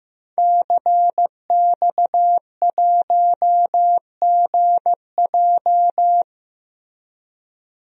Morse code, heard as a keyed tone, about 15 wpm, 700 Hz. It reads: CX1GJ
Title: DXCC-100 Call Signs - 5 Characters - 15wpm